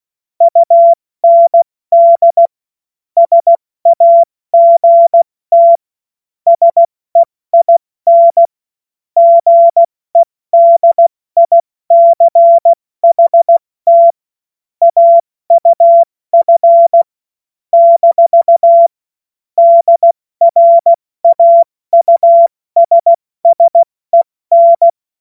Morse code, heard as a keyed tone, about 16 wpm, 680 Hz. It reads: UND SAGT SEIN GEDICHT AUF - DRAUSSEN
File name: Morsekurs / Witze / Witz 14. März 2023 70 F80